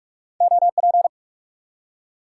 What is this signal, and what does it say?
Morse code, heard as a keyed tone, about 43 wpm, 700 Hz. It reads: OP
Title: Top 90 QSO Elements 45wpm